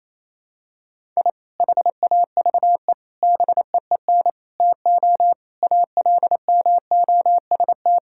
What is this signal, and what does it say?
Morse code, heard as a keyed tone, about 28 wpm, 700 Hz. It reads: I HAVE BEEN TO ALMOST
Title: Sentences from Top 1000 Words - Review Part 5 - 28wpm